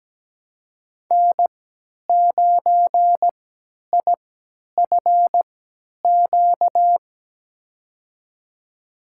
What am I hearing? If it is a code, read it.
Morse code, heard as a keyed tone, about 17 wpm, spaced as for 10 wpm, 700 Hz. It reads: N9IFQ
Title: USA Call Signs - 5 Characters - 3x Character Spacing - 17wpm